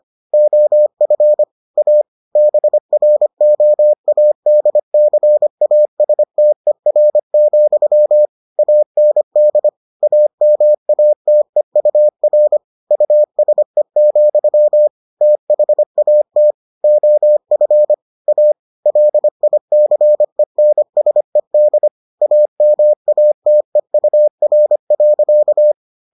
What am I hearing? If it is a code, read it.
Morse code, heard as a keyed tone, about 25 wpm, 600 Hz. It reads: OF A BROADCASTER, AND AMATEUR USE, THAT OF A LICENSED AMATEUR.